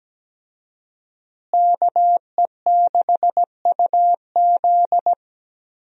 Morse code, heard as a keyed tone, about 17 wpm, 700 Hz. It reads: KE6UZ
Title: USA Call Signs - 5 Characters - Repeated 3x - 17wpm